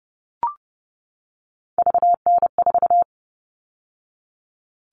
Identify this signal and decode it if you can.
Morse code, heard as a keyed tone, about 30 wpm, 700 Hz. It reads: VN4
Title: All Cut Numbers - 3-Digit 30wpm